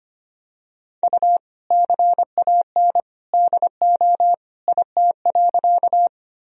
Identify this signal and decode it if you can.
Morse code, heard as a keyed tone, about 25 wpm, 700 Hz. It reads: U CAN DO IT.
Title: Sentences from Top 100 Words and Abbreviations 25wpm